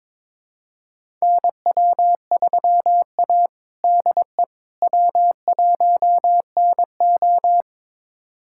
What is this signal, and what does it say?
Morse code, heard as a keyed tone, about 21 wpm, 700 Hz. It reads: NW3A DE W1NO
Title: USA Callsign DE Callsign - 4 Characters - 22wpm